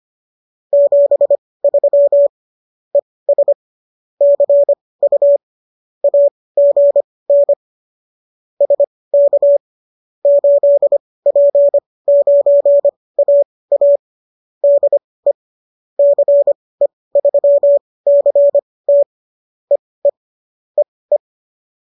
Morse code, heard as a keyed tone, about 25 wpm, spaced as for 18 wpm, 570 Hz. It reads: 73 ES CU AGN SK 8P9AA DE CE3CT EE EE